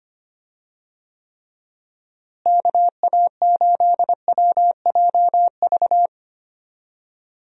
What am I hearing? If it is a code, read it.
Morse code, heard as a keyed tone, about 25 wpm, 700 Hz. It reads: KA8WJV
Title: USA Call Signs - 6 Characters - Repeated 2x - 25wpm